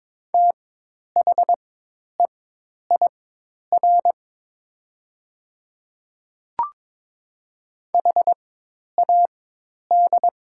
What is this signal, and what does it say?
Morse code, heard as a keyed tone, about 22 wpm, spaced as for 10 wpm, 700 Hz. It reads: THEIR HAD
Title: 4x Character Spacing - Top 100 Words - 22wpm